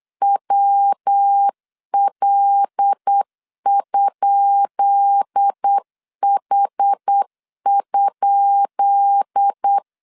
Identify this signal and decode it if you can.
Morse code, heard as a keyed tone, about 8 wpm, 800 Hz. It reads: WL?H?
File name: LEZIONI / 10 SIGNS / 10B INT 8wpm 3534